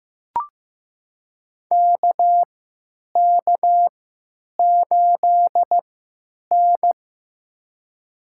Morse code, heard as a keyed tone, about 15 wpm, spaced as for 8 wpm, 700 Hz. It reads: KK8N
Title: USA Call Signs - 4 Characters - 3x Character Spacing - 15wpm